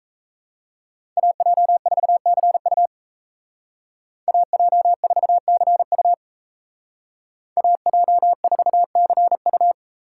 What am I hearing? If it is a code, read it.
Morse code, heard as a keyed tone, about 36 wpm, 700 Hz. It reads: AJ4CU AJ4CU AJ4CU